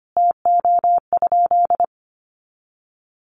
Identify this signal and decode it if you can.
Morse code, heard as a keyed tone, about 25 wpm, 700 Hz. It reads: TO?
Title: Sentences from Top 500 Words 25wpm